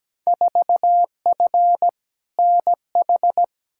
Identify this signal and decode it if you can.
Morse code, heard as a keyed tone, about 17 wpm, 700 Hz. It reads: 4F NH
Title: Field Day Exchanges Encore - 17wpm